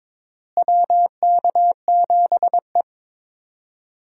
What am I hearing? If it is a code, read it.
Morse code, heard as a keyed tone, about 21 wpm, 700 Hz. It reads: WK7E
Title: USA Call Signs - 4 Characters - Repeated 3x - 22wpm